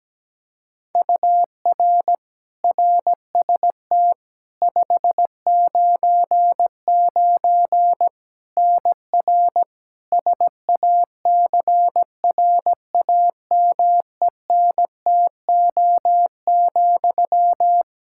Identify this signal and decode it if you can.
Morse code, heard as a keyed tone, about 17 wpm, 700 Hz. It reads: UR RST 599 NR SACRAMENTO,
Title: Practice QSO - Exchange 1 - Beginning - 17wpm